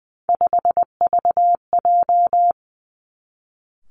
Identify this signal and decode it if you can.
Morse code, heard as a keyed tone, about 20 wpm, 700 Hz. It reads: 5VJ
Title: Three Challenging Letter-Number Combinations 20wpm